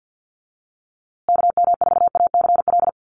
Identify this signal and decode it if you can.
Morse code, heard as a keyed tone, about 50 wpm, 700 Hz. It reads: KM4ACL
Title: USA Call Signs Repeated 3x 50wpm